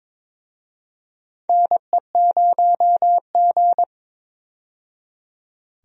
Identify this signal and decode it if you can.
Morse code, heard as a keyed tone, about 22 wpm, 700 Hz. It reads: NE0G
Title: USA Call Signs - 4 Characters - 22wpm